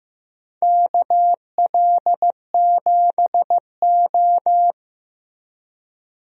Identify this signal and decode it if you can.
Morse code, heard as a keyed tone, about 15 wpm, 700 Hz. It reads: KL7O
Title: DXCC-100 Call Signs - 4 Characters - Repeated 3x - 15wpm